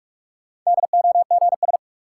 Morse code, heard as a keyed tone, about 41 wpm, 700 Hz. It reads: DOGS
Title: Sentences from Top 1600 Words - New Word Drill Alt - 45wpm